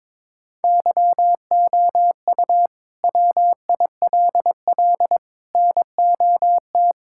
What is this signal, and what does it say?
Morse code, heard as a keyed tone, about 22 wpm, 700 Hz. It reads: YOU WILL NOT
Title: Sentences from Top 1600 Words 22wpm